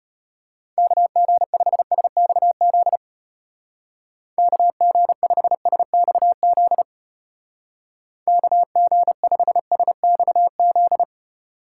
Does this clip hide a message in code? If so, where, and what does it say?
Yes: Morse code from 0.8 to 11.0 s, heard as a keyed tone, about 32 wpm, 700 Hz. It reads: KG5SXZ KG5SXZ KG5SXZ